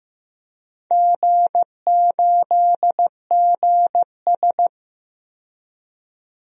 Morse code, heard as a keyed tone, about 15 wpm, 700 Hz. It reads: G8GS
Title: DXCC-100 Call Signs - 4 Characters - 15wpm